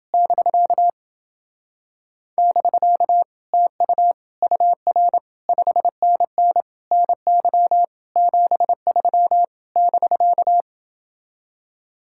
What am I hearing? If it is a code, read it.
Morse code, heard as a keyed tone, about 30 wpm, 700 Hz. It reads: <BK> <BK> TU UR 5NN NY 73 <BK>